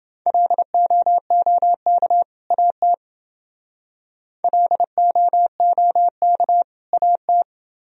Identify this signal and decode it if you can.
Morse code, heard as a keyed tone, about 28 wpm, 700 Hz. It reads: LOOK AT LOOK AT